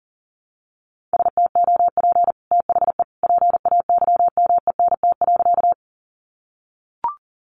Morse code, heard as a keyed tone, about 40 wpm, 700 Hz. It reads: STOP THE PAYMENT.
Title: Sentences from Top 1200 Words 40wpm